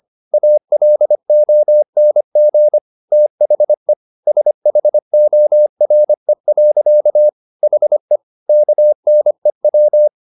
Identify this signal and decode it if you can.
Morse code, heard as a keyed tone, about 25 wpm, 600 Hz. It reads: ALONG THE SHORE. HE KNEW